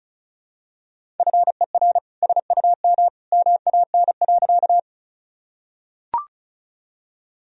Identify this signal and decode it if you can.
Morse code, heard as a keyed tone, about 36 wpm, 700 Hz. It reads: FER SUM MAN.
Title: Sentences from Top 200 Words and Abbreviations 35wpm